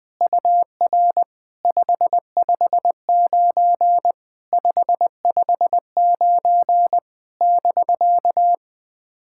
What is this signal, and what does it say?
Morse code, heard as a keyed tone, about 21 wpm, 700 Hz. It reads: UR 559 559 <BK>
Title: POTA QSO Practice - 20wpm